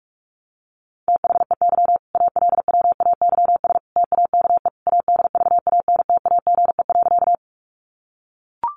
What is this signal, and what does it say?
Morse code, heard as a keyed tone, about 44 wpm, 700 Hz. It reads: THEY ALWAYS TAKE ADVANTAGE.
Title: Sentences from Top 900 Words 45wpm